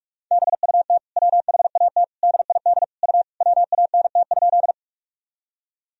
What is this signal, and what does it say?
Morse code, heard as a keyed tone, about 42 wpm, 700 Hz. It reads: BUT WHAT DID U WANT?
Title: Sentences from Top 200 Words and Abbreviations 45wpm